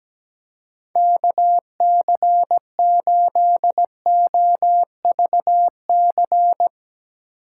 Morse code, heard as a keyed tone, about 17 wpm, 700 Hz. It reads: KC8OVC
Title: USA Call Signs - 6x Character Spacing - 17wpm